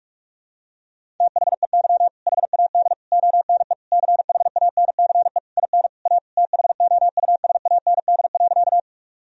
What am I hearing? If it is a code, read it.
Morse code, heard as a keyed tone, about 46 wpm, 700 Hz. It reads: THEY HAD ONE CHANCE IN A THOUSAND.
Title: Sentences from Top 1000 Words - Review Part 3 - 45wpm